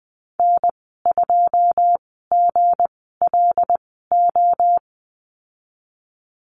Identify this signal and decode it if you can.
Morse code, heard as a keyed tone, about 20 wpm, spaced as for 15 wpm, 700 Hz. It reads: N2GLO
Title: 161 Lesson 26g - Letter g - Call Signs 20wpm